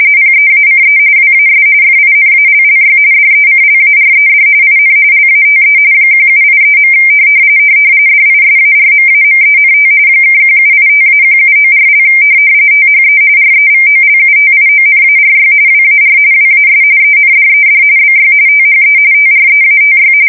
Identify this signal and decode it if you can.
RTTY, about 45.45 baud. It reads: PEACE WAS FINALLY DECLARED IN LATE 1783. DISARMAMENT (1785-1794) THE REVOLUTIONARY WAR WAS ENDED BY THE TREATY
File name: fsk170 / usnavy1